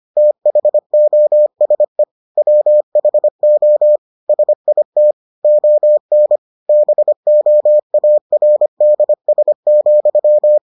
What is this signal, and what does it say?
Morse code, heard as a keyed tone, about 25 wpm, 600 Hz. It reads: THOSE WHO SIT ON BOARDS,